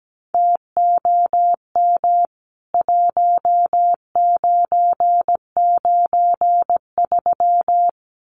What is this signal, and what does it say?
Morse code, heard as a keyed tone, about 17 wpm, 700 Hz. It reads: TOM 1993
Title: CWT Exchange 17wpm